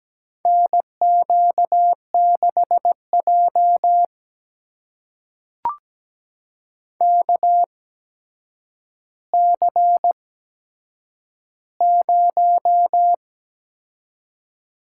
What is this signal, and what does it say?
Morse code, heard as a keyed tone, about 17 wpm, spaced as for 8 wpm, 700 Hz. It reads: NQ6J KC0
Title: USA Call Signs - 8x Character Spacing - 17wpm